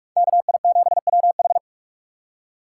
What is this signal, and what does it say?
Morse code, heard as a keyed tone, about 44 wpm, 700 Hz. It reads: KI7WH